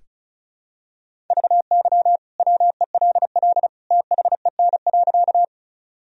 Morse code, heard as a keyed tone, about 36 wpm, 700 Hz. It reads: VY WELL THEN.
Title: Sentences from Top 200 Words and Abbreviations 35wpm